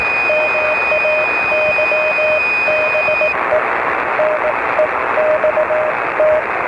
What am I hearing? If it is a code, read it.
Morse code, heard as a keyed tone, about 18 wpm, 620 Hz, with radio noise. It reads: MAYBE NEXT